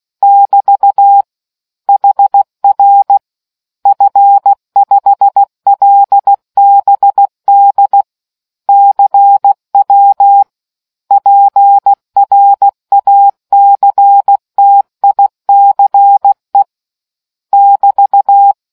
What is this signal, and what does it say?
Morse code, heard as a keyed tone, about 16 wpm, 800 Hz. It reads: <BT> HR F5LBD CW PRACTICE <BT>